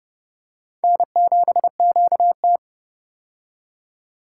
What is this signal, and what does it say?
Morse code, heard as a keyed tone, about 30 wpm, 700 Hz. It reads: N7QT
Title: K1USN SST QSO Practice - 30wpm